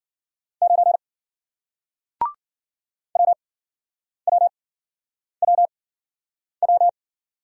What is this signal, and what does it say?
Morse code, heard as a keyed tone, about 51 wpm, spaced as for 9 wpm, 700 Hz. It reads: 9 WWWW